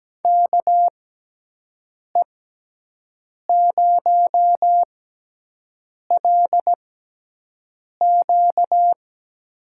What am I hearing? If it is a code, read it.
Morse code, heard as a keyed tone, about 17 wpm, spaced as for 6 wpm, 700 Hz. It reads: KE0LQ
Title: USA Call Signs - 6x Character Spacing - 17wpm